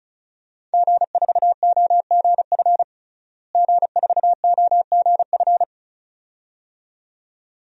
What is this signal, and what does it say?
Morse code, heard as a keyed tone, about 34 wpm, 700 Hz. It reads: G4OGF G4OGF
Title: DXCC-100 Call Signs - 5 Characters - Repeated 2x - 35wpm